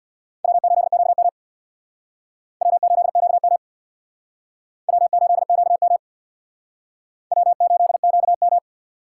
Morse code, heard as a keyed tone, about 64 wpm, 700 Hz. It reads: W8QM W8QM W8QM W8QM